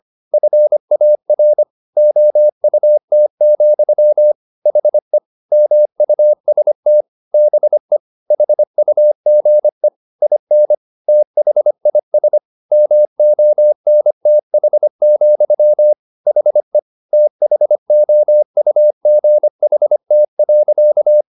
Morse code, heard as a keyed tone, about 25 wpm, 600 Hz. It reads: FAR OUT, HE MUST BE HUGE IN THIS MONTH, HE THOUGHT.
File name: the old man and the sea part 1 25wpm